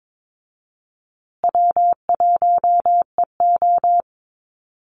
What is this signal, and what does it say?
Morse code, heard as a keyed tone, about 22 wpm, 700 Hz. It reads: W1EO
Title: USA Call Signs Repeated 3x 22wpm